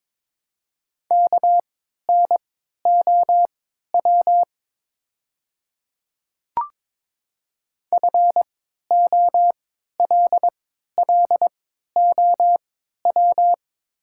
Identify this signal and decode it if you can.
Morse code, heard as a keyed tone, about 21 wpm, spaced as for 12 wpm, 700 Hz. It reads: KNOW FOLLOW
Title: 3x Character Spacing - Top 300 Words - 22wpm